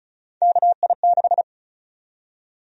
Morse code, heard as a keyed tone, about 36 wpm, 700 Hz. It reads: KI6
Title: USA Call Sign Prefixes - 35wpm